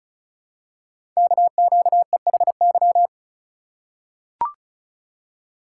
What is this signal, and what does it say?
Morse code, heard as a keyed tone, about 36 wpm, 700 Hz. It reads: KQEHY
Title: Code Groups - 5 Letters - 35wpm